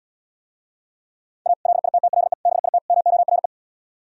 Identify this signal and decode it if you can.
Morse code, heard as a keyed tone, about 77 wpm, 700 Hz. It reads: A LITTLE BIT MORE